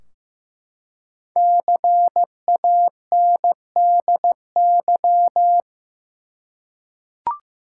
Morse code, heard as a keyed tone, about 15 wpm, 700 Hz. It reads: CANDY
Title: Five Letter Words - 15wpm